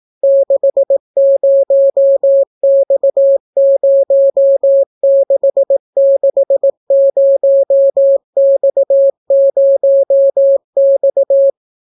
Morse code, heard as a keyed tone, about 18 wpm, 550 Hz. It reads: 60X0660X0X